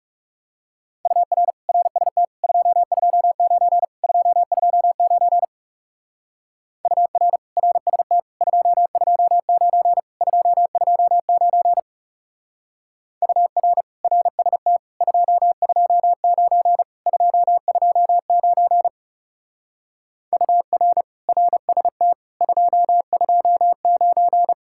Morse code, heard as a keyed tone, about 37 wpm, 700 Hz. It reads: UR RST 229 229 UR RST 229 229 UR RST 229 229 UR RST 229
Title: Speed-Racing Common QSO Phrases - 30wpm